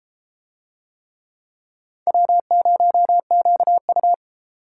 Morse code, heard as a keyed tone, about 31 wpm, 700 Hz. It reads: W0QU